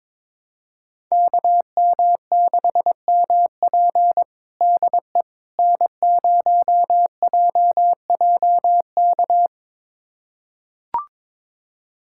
Morse code, heard as a keyed tone, about 22 wpm, 700 Hz. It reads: KM6MP DE N0JJK
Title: USA Callsign DE Callsign - 5 Characters - 22wpm